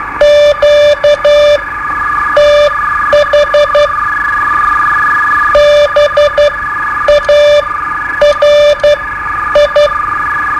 Morse code, heard as a keyed tone, about 12 wpm, spaced as for 9 wpm, 590 Hz. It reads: QTH BARI
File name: qso-01